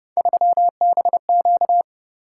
Morse code, heard as a keyed tone, about 30 wpm, 700 Hz. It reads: 3BQ